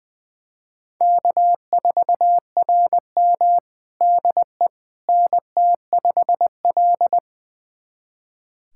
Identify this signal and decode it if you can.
Morse code, heard as a keyed tone, about 20 wpm, 700 Hz. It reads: K4RM DE NT5L
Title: USA Callsign DE Callsign - 4 Characters - 20wpm